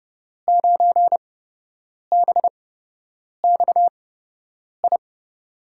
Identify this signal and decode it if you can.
Morse code, heard as a keyed tone, about 30 wpm, spaced as for 8 wpm, 700 Hz. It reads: 9BXI
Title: USA Call Signs - 6 Characters - 8x Character Spacing - 30wpm